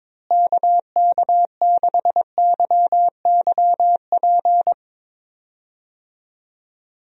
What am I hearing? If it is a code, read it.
Morse code, heard as a keyed tone, about 22 wpm, 700 Hz. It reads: KK6YYP
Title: USA Call Signs - 6 Characters - Repeated 2x - 22wpm